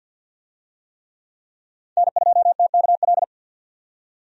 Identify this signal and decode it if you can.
Morse code, heard as a keyed tone, about 50 wpm, 700 Hz. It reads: N2TKL
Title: USA Call Signs - 5 Characters - 50wpm